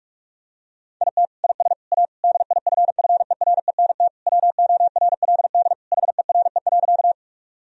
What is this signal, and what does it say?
Morse code, heard as a keyed tone, about 46 wpm, 700 Hz. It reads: IT IS A DIFFERENT WORLD HERE.
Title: Sentences from Top 1000 Words - Review Part 3 - 45wpm